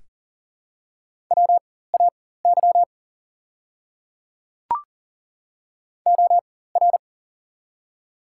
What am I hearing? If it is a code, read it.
Morse code, heard as a keyed tone, about 40 wpm, spaced as for 19 wpm, 700 Hz. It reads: WAY OR
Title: 4x Character Spacing - Top 100 Words - 40wpm